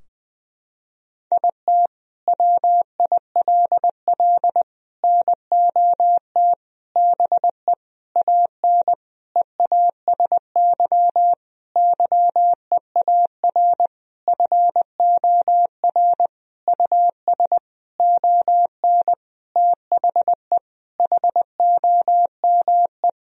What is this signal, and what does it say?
Morse code, heard as a keyed tone, about 20 wpm, 700 Hz. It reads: IT WILL NOT BE AN EASY YEAR FOR US ON THE HOME
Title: Sentences from Top 1000 Words - Review Part 2 - 20wpm